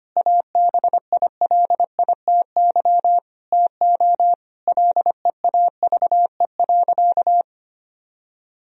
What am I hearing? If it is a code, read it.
Morse code, heard as a keyed tone, about 25 wpm, 700 Hz. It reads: ABILITY TO LEAVE.
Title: Sentences from Top 1000 Words - Review Part 3 - 25wpm